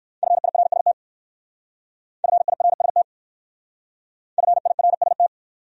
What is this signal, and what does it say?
Morse code, heard as a keyed tone, about 56 wpm, 700 Hz. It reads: FIRST FIRST FIRST
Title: Speed-Racing Top 200 Words 45wpm